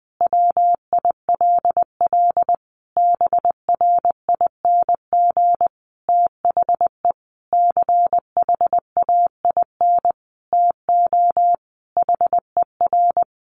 Morse code, heard as a keyed tone, about 20 wpm, 700 Hz. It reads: WILL BRING THE CHAIN TO HER
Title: Sentences from Top 1500 Words 20wpm